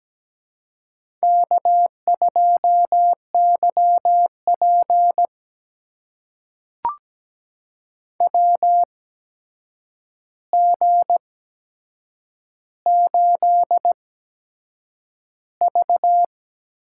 Morse code, heard as a keyed tone, about 17 wpm, spaced as for 7 wpm, 700 Hz. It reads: K2YP WG8V
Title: USA Call Signs - 4 Characters - 8x Character Spacing - 17wpm